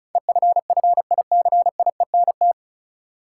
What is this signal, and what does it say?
Morse code, heard as a keyed tone, about 36 wpm, 700 Hz. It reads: EFFICIENT
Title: Sentences from Top 1300 Words - New Word Drill - 35wpm